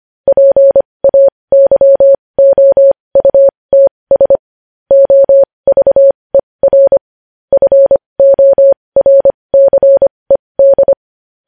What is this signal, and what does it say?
Morse code, heard as a keyed tone, about 25 wpm, spaced as for 20 wpm, 550 Hz. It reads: PAYOUTS OVER FORCED
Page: News Headlines in Morse Code at 20 WPM